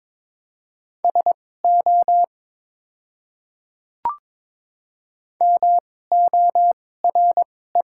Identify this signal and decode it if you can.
Morse code, heard as a keyed tone, about 21 wpm, spaced as for 16 wpm, 700 Hz. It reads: SO MORE
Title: 2x Character Spacing - Top 100 Words - 22wpm